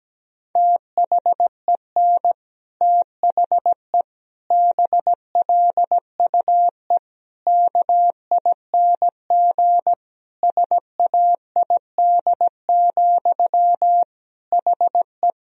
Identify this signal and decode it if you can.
Morse code, heard as a keyed tone, about 17 wpm, 700 Hz. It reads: THEN THE BLUE KING SAID, HE
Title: Sentences from Top 1000 Words - Review Part 3 - 17wpm